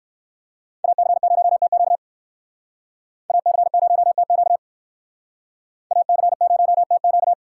Morse code, heard as a keyed tone, about 60 wpm, 700 Hz. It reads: AC0TQ AC0TQ AC0TQ